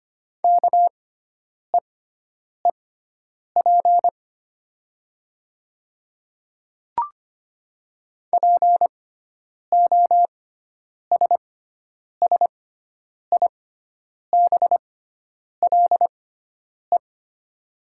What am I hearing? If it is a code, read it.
Morse code, heard as a keyed tone, about 25 wpm, spaced as for 9 wpm, 700 Hz. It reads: KEEP POSSIBLE
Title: 6x Character Spacing - Top 500 Words - 25wpm